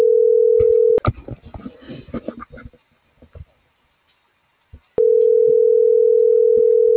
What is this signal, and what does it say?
Ringback tone (North America), 440 + 480 Hz.